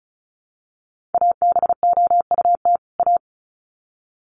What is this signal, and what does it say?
Morse code, heard as a keyed tone, about 34 wpm, 700 Hz. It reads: ABOUT A
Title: Sets of 2 Words 35wpm